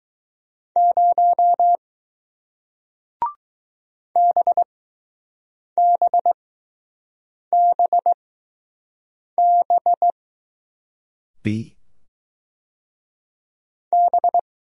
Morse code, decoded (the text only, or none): 0 BBBB B